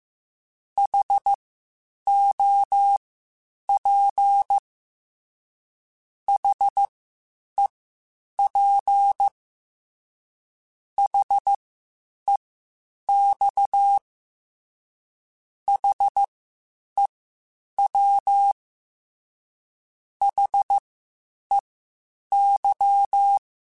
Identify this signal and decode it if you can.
Morse code, heard as a keyed tone, about 15 wpm, spaced as for 8 wpm, 800 Hz. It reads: HOP HEP HEX HEW HEY
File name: H Prefixes / H mp3s